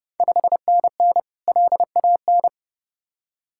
Morse code, heard as a keyed tone, about 30 wpm, 700 Hz. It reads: 5NN LAN
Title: Ontario QSO Party Exchange 30wpm